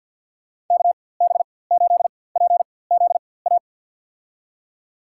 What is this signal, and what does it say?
Morse code, heard as a keyed tone, about 50 wpm, spaced as for 23 wpm, 700 Hz. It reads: KB8PZA
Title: USA Call Signs - 6 Characters - 4x Character Spacing - 50wpm